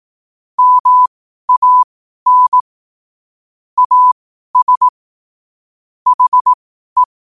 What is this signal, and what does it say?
Morse code, heard as a keyed tone, about 18 wpm, spaced as for 12 wpm, 1000 Hz. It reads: MAN AS HE
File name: qotdfiles / 12wpm